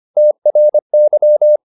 Morse code, heard as a keyed tone, about 25 wpm, 600 Hz. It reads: TRY